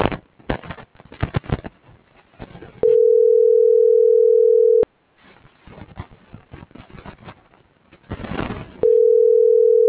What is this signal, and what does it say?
Ringback tone (North America), 440 + 480 Hz.